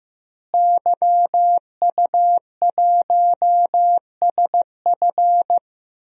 Morse code, heard as a keyed tone, about 15 wpm, 700 Hz. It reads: YU1SF
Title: DXCC-100 Call Signs - 5 Characters - Repeated 2x - 15wpm